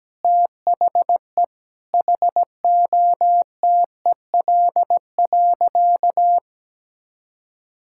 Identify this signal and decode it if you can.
Morse code, heard as a keyed tone, about 17 wpm, 700 Hz. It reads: THE HOTEL.
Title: Sentences from Top 1600 Words 17wpm